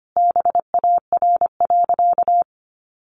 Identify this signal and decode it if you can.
Morse code, heard as a keyed tone, about 25 wpm, 700 Hz. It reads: BAR.
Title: Easy Phrases 25wpm